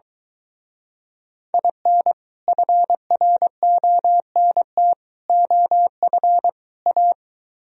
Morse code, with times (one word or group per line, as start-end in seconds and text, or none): 1.5-2.1 IN
2.5-4.9 FRONT
5.3-6.5 OF
6.9-7.1 A